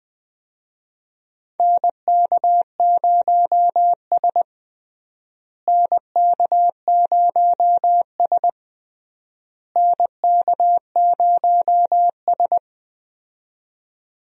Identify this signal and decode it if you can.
Morse code, heard as a keyed tone, about 21 wpm, 700 Hz. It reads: NK0S NK0S NK0S
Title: USA Call Signs - 4 Characters - Repeated 3x - 20wpm